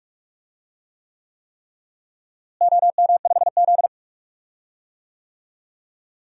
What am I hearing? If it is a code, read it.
Morse code, heard as a keyed tone, about 47 wpm, 700 Hz. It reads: OM5Z
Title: DXCC-100 Call Signs - 4 Characters - Repeated 3x - 45wpm